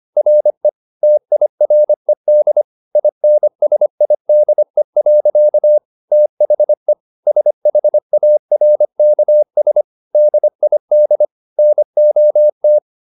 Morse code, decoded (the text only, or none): RE TIRED INSIDE. THE SHARKS DID NOT